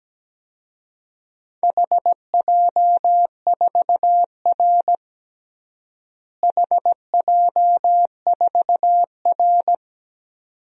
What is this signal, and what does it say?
Morse code, heard as a keyed tone, about 17 wpm, 700 Hz. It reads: HJ4R HJ4R
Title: DXCC-100 Call Signs - 4 Characters - Repeated 2x - 17wpm